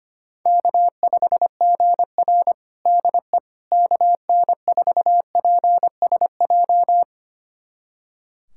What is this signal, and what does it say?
Morse code, heard as a keyed tone, about 25 wpm, 700 Hz. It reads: K5GR DE KN4PSJ
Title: USA Callsign DE Callsign 25wpm